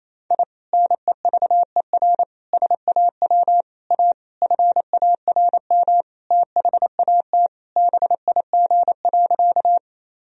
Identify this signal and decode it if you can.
Morse code, heard as a keyed tone, about 28 wpm, 700 Hz. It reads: I NEVER SAW A FARM THAT BIG.
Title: Sentences from Top 1000 Words - Review Part 2 - 28wpm